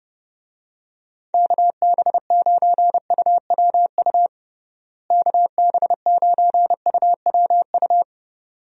Morse code, heard as a keyed tone, about 30 wpm, 700 Hz. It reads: KB9UWU KB9UWU
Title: USA Call Signs - 6 Characters - Repeated 2x - 30wpm